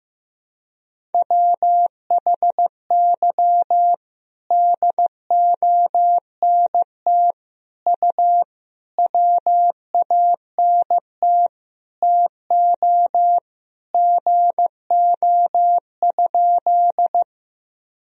Morse code, heard as a keyed tone, about 15 wpm, 700 Hz. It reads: WHY DONT U WANT TO GO?
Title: Sentences from Top 300 Words and Abbreviations 15wpm